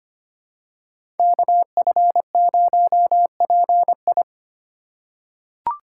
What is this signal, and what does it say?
Morse code, heard as a keyed tone, about 25 wpm, 700 Hz. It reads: KF0PI
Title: USA Call Signs - 5 Characters - 4x Character Spacing - 25wpm